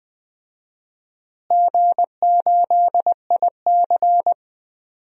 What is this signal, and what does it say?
Morse code, heard as a keyed tone, about 20 wpm, 700 Hz. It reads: G8IC
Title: DXCC-100 Call Signs - 4 Characters - Repeated 2x - 20wpm